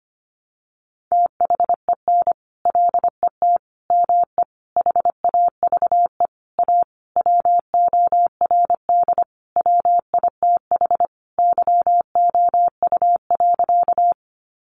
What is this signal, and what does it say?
Morse code, heard as a keyed tone, about 25 wpm, 700 Hz. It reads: THEN LET ME HAVE A WORD WITH YOU.
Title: Sentences from Top 300 Words 25wpm